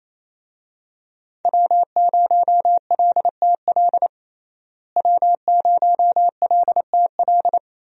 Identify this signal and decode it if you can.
Morse code, heard as a keyed tone, about 27 wpm, 700 Hz. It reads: W0LTL W0LTL
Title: USA Call Signs - 5 Characters - Repeated 3x - 28wpm